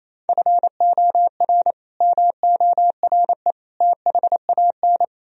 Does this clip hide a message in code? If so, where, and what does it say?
Yes: Morse code from 0.3 to 5.0 s, heard as a keyed tone, about 36 wpm, 700 Hz. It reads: FOR MORE THAN